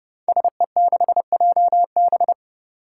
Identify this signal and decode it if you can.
Morse code, heard as a keyed tone, about 32 wpm, 700 Hz. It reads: SE6JB
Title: Five Challenging Letter-Number Combinations 30wpm